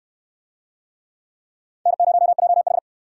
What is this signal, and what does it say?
Morse code, heard as a keyed tone, about 69 wpm, 700 Hz. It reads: N0JH